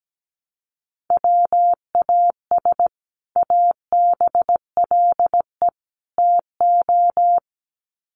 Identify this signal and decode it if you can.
Morse code, heard as a keyed tone, about 17 wpm, 700 Hz. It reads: WAS ABLE TO
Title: Sets of 3 Words 17wpm